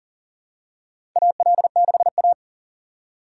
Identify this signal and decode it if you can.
Morse code, heard as a keyed tone, about 40 wpm, 700 Hz. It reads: AL6A
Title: USA Call Signs - 4 Characters - 2x Character Spacing - 40wpm